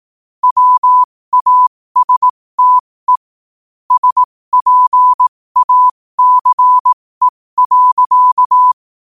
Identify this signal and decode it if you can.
Morse code, heard as a keyed tone, about 18 wpm, spaced as for 15 wpm, 1000 Hz. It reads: WASTE SPACE.
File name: qotdfiles / 15wpm